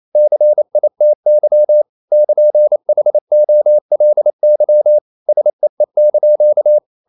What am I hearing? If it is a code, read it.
Morse code, heard as a keyed tone, about 28 wpm, 600 Hz. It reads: CITY <KN>HOLY SEE)
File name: CW / Practice / NationalCapitals28